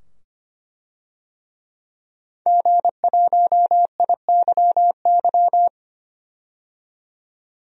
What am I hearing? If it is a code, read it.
Morse code, heard as a keyed tone, about 25 wpm, 700 Hz. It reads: G1IYY
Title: DXCC-100 Call Signs - 5 Characters - 25wpm